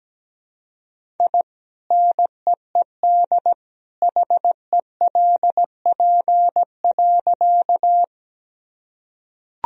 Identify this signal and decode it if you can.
Morse code, heard as a keyed tone, about 17 wpm, 700 Hz. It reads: I NEED HELP.
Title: Sentences from Top 1000 Words - Review Part 1 - 17wpm